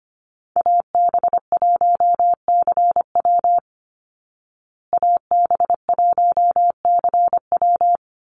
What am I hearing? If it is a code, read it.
Morse code, heard as a keyed tone, about 25 wpm, 700 Hz. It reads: AB1CW AB1CW
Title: Call Signs Repeated 3x - 25wpm